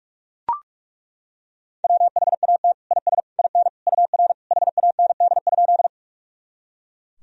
Morse code, heard as a keyed tone, about 41 wpm, 700 Hz. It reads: WHAT IS IN UR HAND?
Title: Sentences from Top 200 Words and Abbreviations 45wpm